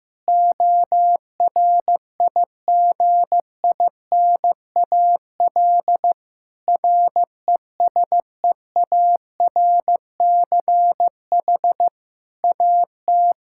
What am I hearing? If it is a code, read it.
Morse code, heard as a keyed tone, about 15 wpm, 700 Hz. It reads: ORIGINAL RESEARCH AT
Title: Sentences from Top 1000 Words - Review Part 6 - 15wpm